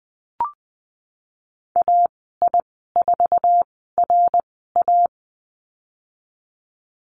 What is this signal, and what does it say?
Morse code, heard as a keyed tone, about 20 wpm, spaced as for 15 wpm, 700 Hz. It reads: AI4RA
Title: 055 Lesson 10g - Letter r - Call Signs 20wpm